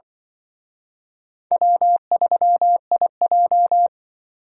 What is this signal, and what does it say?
Morse code, heard as a keyed tone, about 24 wpm, 700 Hz. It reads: W3IJ
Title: Speed Racing - USA Call Signs 20wpm